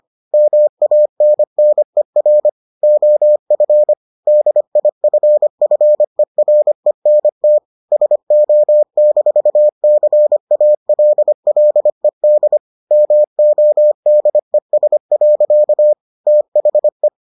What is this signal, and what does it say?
Morse code, heard as a keyed tone, about 25 wpm, 600 Hz. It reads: MANNER OF DIFFERENT SO-CALLED MODES. THE